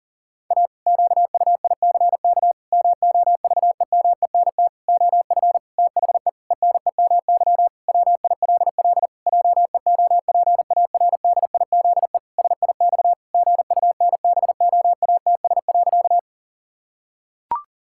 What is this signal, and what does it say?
Morse code, heard as a keyed tone, about 40 wpm, 700 Hz. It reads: A QUICK MOVEMENT OF THE ENEMY WILL JEOPARDIZE SIX GUNBOATS.
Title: Pangram Challenge v2 - 40wpm